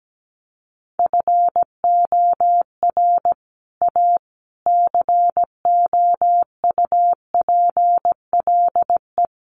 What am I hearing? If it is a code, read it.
Morse code, heard as a keyed tone, about 17 wpm, 700 Hz. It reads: FOR A COUPLE
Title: Sets of 5 Words Encore2 17wpm